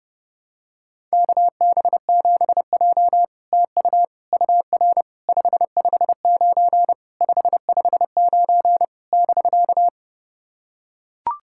Morse code, heard as a keyed tone, about 30 wpm, 700 Hz. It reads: KB7J TU UR 559 559 <BK>
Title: SOTA QSO Practice 30wpm